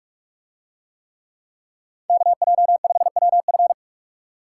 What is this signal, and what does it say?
Morse code, heard as a keyed tone, about 45 wpm, 700 Hz. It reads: KJ5WF